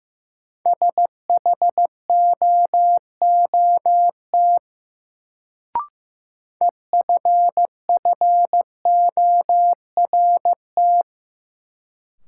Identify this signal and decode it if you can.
Morse code, heard as a keyed tone, about 15 wpm, 700 Hz. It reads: SHOOT EFFORT
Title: Sentences from Top 1200 Words - New Word Drill - 15wpm